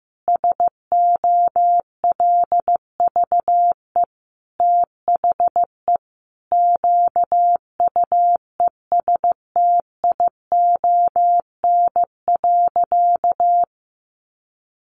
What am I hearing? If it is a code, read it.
Morse code, heard as a keyed tone, about 15 wpm, 700 Hz. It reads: SOLVE THE QUESTION.
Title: Sentences from Top 1000 Words 15wpm